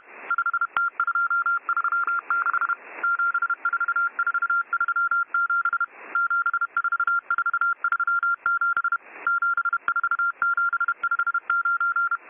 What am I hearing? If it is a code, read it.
Morse code, heard as a keyed tone, about 31 wpm, 1390 Hz. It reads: 5T236 74437 74437 74759